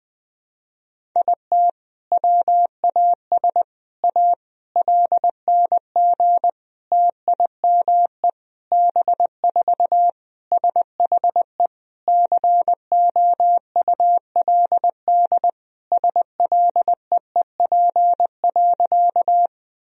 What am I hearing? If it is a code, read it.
Morse code, heard as a keyed tone, about 21 wpm, 700 Hz. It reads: IT WAS A LNG TIME B4 SHE COULD SLEEP.
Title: Sentences from Top 500 Words and Abbreviations 20wpm